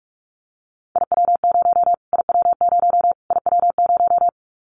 Morse code, heard as a keyed tone, about 44 wpm, 700 Hz. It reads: IW0 IW0 IW0
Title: Callsign Prefixes for Top 25 DXCC - Repeated 3x - 45wpm